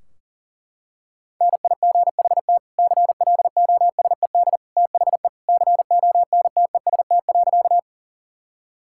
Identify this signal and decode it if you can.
Morse code, heard as a keyed tone, about 40 wpm, 700 Hz. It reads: NIGHT CLOSED THE CONTEST.
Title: Sentences from Top 1600 Words 40wpm